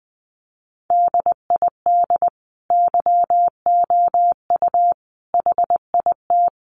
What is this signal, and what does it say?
Morse code, heard as a keyed tone, about 20 wpm, 700 Hz. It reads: DID YOU HIT
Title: Sentences from Top 1400 Words 20wpm